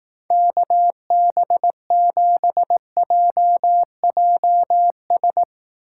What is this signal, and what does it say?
Morse code, heard as a keyed tone, about 18 wpm, 700 Hz. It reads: KB7JJS
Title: Speed Racing - USA 2x3 Call Signs 15wpm